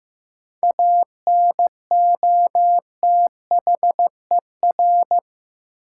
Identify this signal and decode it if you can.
Morse code, heard as a keyed tone, about 15 wpm, 700 Hz. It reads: ANOTHER
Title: Sets of 5 Words Encore - New Word Drill - 15wpm